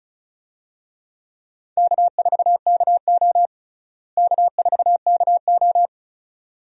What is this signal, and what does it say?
Morse code, heard as a keyed tone, about 36 wpm, 700 Hz. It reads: K4KO K4KO
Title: USA Call Signs - 4 Characters - Repeated 2x - 35wpm